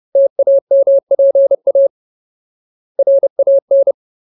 Morse code, heard as a keyed tone, about 30 wpm, 550 Hz. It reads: TAMPA RAN
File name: Ultimate Word List / 30WPM Ultimate